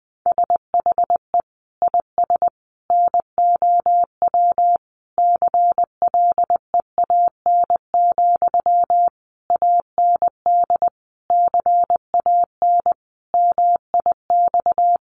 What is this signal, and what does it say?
Morse code, heard as a keyed tone, about 20 wpm, 700 Hz. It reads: SHE IS NOW CLEAN, AND CAN MIX
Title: Sentences from Top 1100 Words 20wpm